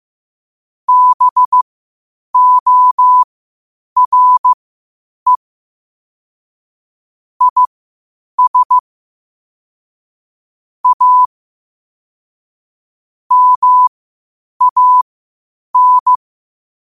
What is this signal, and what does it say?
Morse code, heard as a keyed tone, about 15 wpm, spaced as for 8 wpm, 1000 Hz. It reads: BORE IS A MAN
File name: 8wpm